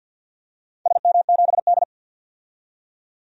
Morse code, heard as a keyed tone, about 52 wpm, 700 Hz. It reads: SM7D
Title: DXCC-100 Call Signs - 4 Characters - 50wpm